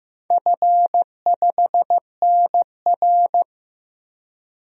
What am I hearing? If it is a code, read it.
Morse code, heard as a keyed tone, about 15 wpm, 700 Hz. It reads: F5NR
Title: DXCC-100 Call Signs - 4 Characters - 15wpm